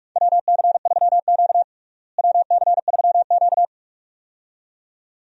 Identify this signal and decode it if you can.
Morse code, heard as a keyed tone, about 46 wpm, 700 Hz. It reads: WC3Q WC3Q
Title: USA Call Signs - 4 Characters - Repeated 2x - 45wpm